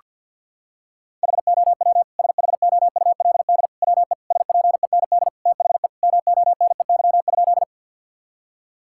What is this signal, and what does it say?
Morse code, heard as a keyed tone, about 48 wpm, 700 Hz. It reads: HOW SHOULD WE SPEND THE MONEY?
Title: Sentences from Top 1000 Words - Review Part 5 - 50wpm